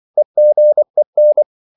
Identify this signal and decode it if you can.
Morse code, heard as a keyed tone, about 24 wpm, 600 Hz. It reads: EGEN
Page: Korte danske ord 24wpm